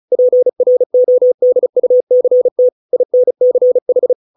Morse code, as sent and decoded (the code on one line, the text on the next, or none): .--. .-. --- -.. ..- -.-. - / .. -. -.-. ....
PRODUCT INCH